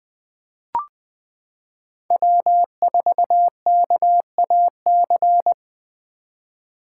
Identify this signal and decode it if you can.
Morse code, heard as a keyed tone, about 20 wpm, 700 Hz. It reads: W4KAC
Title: 2023 SOTA Chaser Call Signs - 20wpm